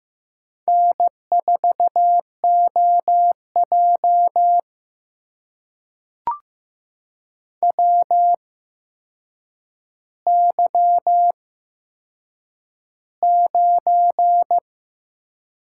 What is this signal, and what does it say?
Morse code, heard as a keyed tone, about 15 wpm, spaced as for 7 wpm, 700 Hz. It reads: N4OJ WY9
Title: USA Call Signs - 4 Characters - 8x Character Spacing - 15wpm